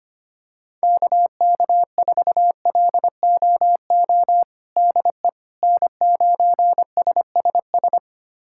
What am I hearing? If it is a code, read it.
Morse code, heard as a keyed tone, about 25 wpm, 700 Hz. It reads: KK4LOO DE N9SSS
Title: USA Callsign DE Callsign 25wpm